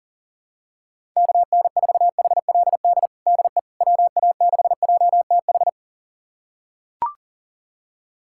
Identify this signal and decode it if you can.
Morse code, heard as a keyed tone, about 40 wpm, 700 Hz. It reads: KN4HLD DE WA6JTH
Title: USA Callsign DE Callsign 40wpm